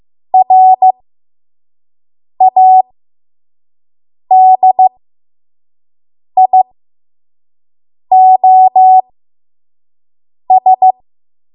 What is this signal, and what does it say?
Morse code, heard as a keyed tone, about 15 wpm, spaced as for 5 wpm, 750 Hz. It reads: RADIOS
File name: Morse / 05WPM